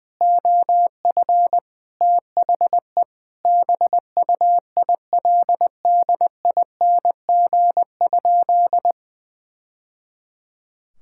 Morse code, heard as a keyed tone, about 20 wpm, 700 Hz. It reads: OF THE BUILDING?
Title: Sentences from Top 1000 Words - Review Part 5 - 20wpm